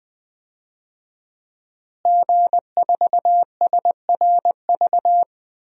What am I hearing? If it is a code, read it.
Morse code, heard as a keyed tone, about 20 wpm, 700 Hz. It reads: G4SRV
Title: DXCC-100 Call Signs - 5 Characters - Repeated 3x - 20wpm